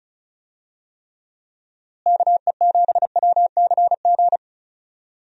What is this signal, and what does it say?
Morse code, heard as a keyed tone, about 36 wpm, 700 Hz. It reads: KE7WCG
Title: USA Call Signs - 6 Characters - Repeated 3x - 35wpm